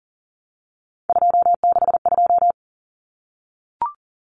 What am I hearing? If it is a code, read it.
Morse code, heard as a keyed tone, about 40 wpm, 700 Hz. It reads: 262
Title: Numbers - 3-Digit 40wpm